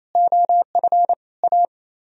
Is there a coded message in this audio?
Yes: Morse code from 0.2 to 1.6 s, heard as a keyed tone, about 36 wpm, 700 Hz. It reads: OF A